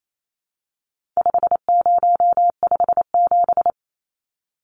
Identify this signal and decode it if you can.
Morse code, heard as a keyed tone, about 28 wpm, 700 Hz. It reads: 5057
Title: Numbers - 4-Digit 28wpm